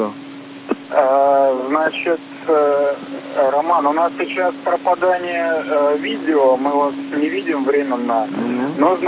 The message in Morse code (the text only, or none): none